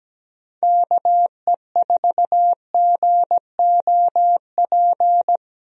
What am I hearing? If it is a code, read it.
Morse code, heard as a keyed tone, about 17 wpm, 700 Hz. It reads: KE4GOP
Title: USA Call Signs - 6 Characters - Repeated 3x - 17wpm